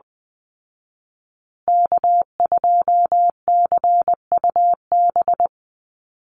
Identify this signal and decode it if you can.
Morse code, heard as a keyed tone, about 20 wpm, 700 Hz. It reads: K2CUB
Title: Call Signs Repeated 3x - 20wpm